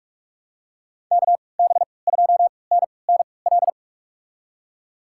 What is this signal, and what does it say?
Morse code, heard as a keyed tone, about 46 wpm, spaced as for 26 wpm, 700 Hz. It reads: KB2NNL
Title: USA Call Signs - 6 Characters - 3x Character Spacing - 45wpm